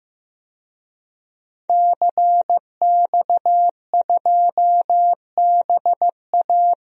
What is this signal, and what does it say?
Morse code, heard as a keyed tone, about 15 wpm, 700 Hz. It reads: CX2BA
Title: DXCC-100 Call Signs - 5 Characters - Repeated 2x - 15wpm